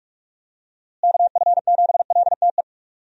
Morse code, heard as a keyed tone, about 42 wpm, 700 Hz. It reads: KF7LTE
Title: USA Call Signs - 6 Characters - Repeated 2x - 45wpm